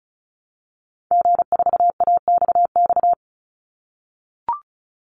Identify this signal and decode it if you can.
Morse code, heard as a keyed tone, about 34 wpm, 700 Hz. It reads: G4AXX
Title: Call Signs - 35wpm